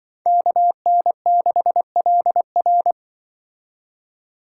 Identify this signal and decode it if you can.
Morse code, heard as a keyed tone, about 24 wpm, 700 Hz. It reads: KN6LR
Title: Speed Racing - USA 2x2 Call Signs 20wpm